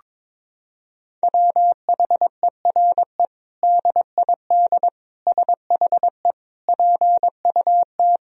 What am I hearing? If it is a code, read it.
Morse code, heard as a keyed tone, about 22 wpm, 700 Hz. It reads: WHERE DID SHE PUT
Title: Sentences from Top 1000 Words - Review Part 2 - 22wpm